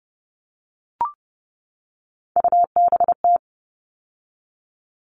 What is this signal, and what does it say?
Morse code, heard as a keyed tone, about 30 wpm, 700 Hz. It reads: UBT
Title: All Cut Numbers - 3-Digit 30wpm